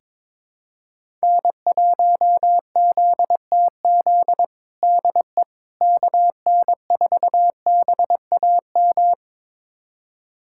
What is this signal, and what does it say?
Morse code, heard as a keyed tone, about 22 wpm, 700 Hz. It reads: N1ZTZ DE KN4BAM
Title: USA Callsign DE Callsign 22wpm